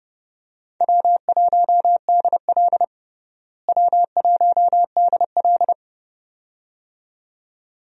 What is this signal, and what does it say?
Morse code, heard as a keyed tone, about 30 wpm, 700 Hz. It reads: W1DL W1DL
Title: USA Call Signs - 4 Characters - Repeated 2x - 30wpm